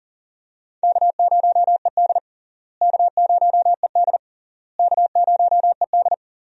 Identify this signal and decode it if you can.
Morse code, heard as a keyed tone, about 40 wpm, 700 Hz. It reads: K0ED K0ED K0ED
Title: USA Call Signs - 4 Characters - Repeated 3x - 40wpm